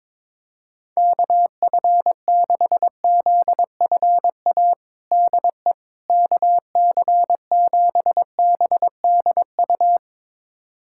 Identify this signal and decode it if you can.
Morse code, heard as a keyed tone, about 22 wpm, 700 Hz. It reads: KF6ZFA DE KC7BDU
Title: USA Callsign DE Callsign - 6 Characters - 22wpm